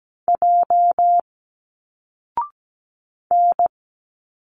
Morse code, heard as a keyed tone, about 17 wpm, 700 Hz. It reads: J N
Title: Single Letters 17wpm